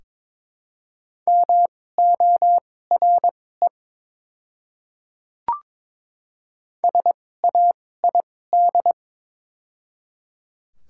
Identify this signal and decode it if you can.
Morse code, heard as a keyed tone, about 22 wpm, spaced as for 16 wpm, 700 Hz. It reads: MORE SAID
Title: 2x Character Spacing - Top 300 Words - 22wpm